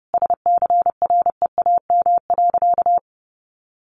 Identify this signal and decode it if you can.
Morse code, heard as a keyed tone, about 30 wpm, 700 Hz. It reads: SCREAM.
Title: A Sound Of Thunder 30wpm